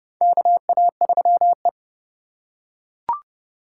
Morse code, heard as a keyed tone, about 30 wpm, 700 Hz. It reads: KA3E
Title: USA Call Signs - 2x Character Spacing - 30wpm